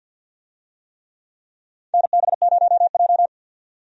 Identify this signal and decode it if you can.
Morse code, heard as a keyed tone, about 53 wpm, 700 Hz. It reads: NB0J
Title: USA Call Signs - 4 Characters - 50wpm